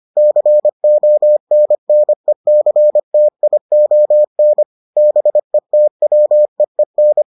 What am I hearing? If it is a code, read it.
Morse code, heard as a keyed tone, about 25 wpm, 600 Hz. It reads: CONNECTION BETWEEN